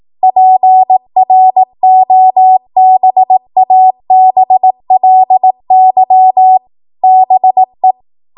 Morse code, heard as a keyed tone, about 18 wpm, 750 Hz. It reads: PROBABLY BE
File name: Morse / 18WPM